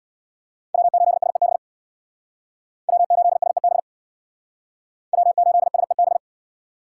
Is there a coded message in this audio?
Yes: Morse code from 0.7 to 6.2 s, heard as a keyed tone, about 65 wpm, 700 Hz. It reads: W8SEB W8SEB W8SEB